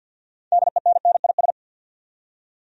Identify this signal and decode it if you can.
Morse code, heard as a keyed tone, about 51 wpm, 700 Hz. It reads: DENNIS
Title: USA Common Names - 50wpm